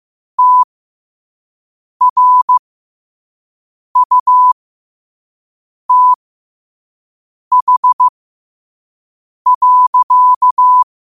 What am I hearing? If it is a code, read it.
Morse code, heard as a keyed tone, about 15 wpm, spaced as for 5 wpm, 1000 Hz. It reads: TRUTH.
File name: qotdfiles / 5wpm